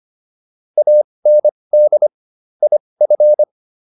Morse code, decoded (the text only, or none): AND IF